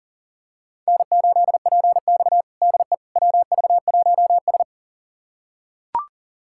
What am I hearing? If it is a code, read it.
Morse code, heard as a keyed tone, about 37 wpm, 700 Hz. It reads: N8PX DE WV1S
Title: USA Callsign DE Callsign - 4 Characters - 40wpm